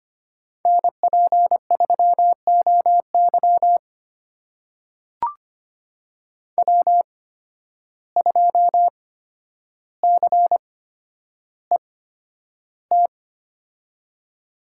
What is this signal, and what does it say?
Morse code, heard as a keyed tone, about 25 wpm, spaced as for 11 wpm, 700 Hz. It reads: NP3OY W2CET
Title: USA Call Signs - 8x Character Spacing - 25wpm